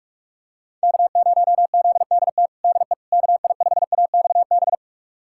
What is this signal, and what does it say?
Morse code, heard as a keyed tone, about 41 wpm, 700 Hz. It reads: K0ZDT DE KI5AXB
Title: USA Callsign DE Callsign 45wpm